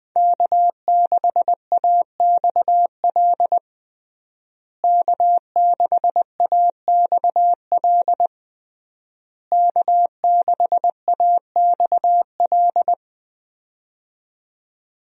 Morse code, heard as a keyed tone, about 21 wpm, 700 Hz. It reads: K6AXL K6AXL K6AXL
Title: USA Call Signs - 5 Characters - Repeated 3x - 20wpm